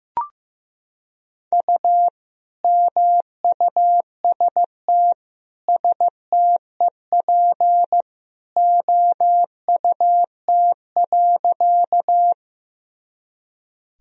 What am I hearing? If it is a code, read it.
Morse code, heard as a keyed tone, about 15 wpm, 700 Hz. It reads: U MUST STEP OUT.
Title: Sentences from Top 400 Words and Abbreviations 15wpm